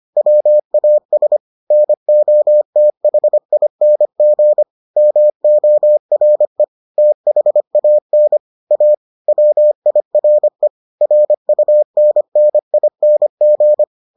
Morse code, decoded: WAS NOTHING MORE THAN A WIRE RUNNING

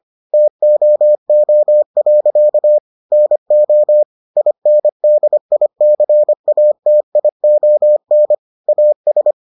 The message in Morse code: TOO. NO INDICATION AS

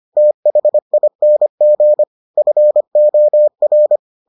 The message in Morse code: THING FOR

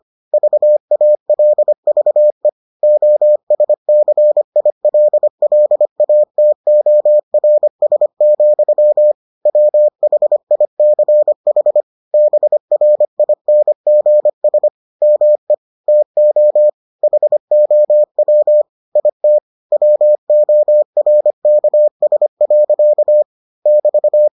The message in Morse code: VALVE OSCILLATORS, WHICH BRINGS ME TO HOW IT WORKS. <BT>

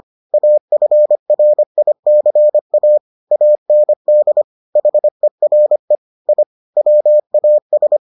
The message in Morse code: AFRICA AND HERE I WAS

